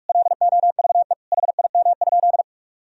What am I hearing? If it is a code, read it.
Morse code, heard as a keyed tone, about 46 wpm, 700 Hz. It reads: LOVE HIM?
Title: Sentences from Top 400 Words and Abbreviations 45wpm